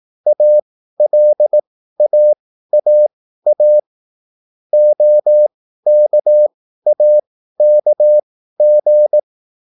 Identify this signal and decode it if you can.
Morse code, heard as a keyed tone, about 18 wpm, spaced as for 13 wpm, 600 Hz. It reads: ALAAA OKAKG